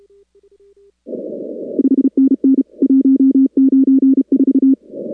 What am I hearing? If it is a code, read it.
Morse code, heard as a keyed tone, about 33 wpm, 270 Hz. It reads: 5NN 194
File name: contests / 2019-cw-wpx-kt5j-k5pi- -2330 0-41